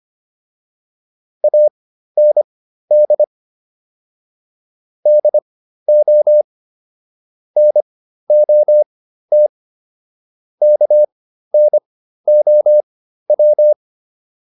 Morse code, heard as a keyed tone, about 25 wpm, spaced as for 13 wpm, 600 Hz. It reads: AND DO NOT KNOW